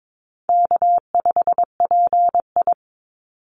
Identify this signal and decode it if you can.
Morse code, heard as a keyed tone, about 22 wpm, 700 Hz. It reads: K5PI
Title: Easy Call Signs Repeated 3x - 22wpm